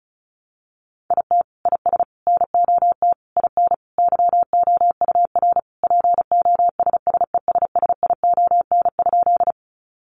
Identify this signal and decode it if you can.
Morse code, heard as a keyed tone, about 34 wpm, 700 Hz. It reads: IT IS NOT IN YOUR POSSESSION?
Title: Sentences from Top 1400 Words 35wpm